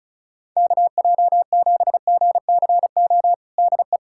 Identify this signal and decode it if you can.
Morse code, heard as a keyed tone, about 36 wpm, 700 Hz. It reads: KJ7GCO DE
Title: USA Callsign DE Callsign 35wpm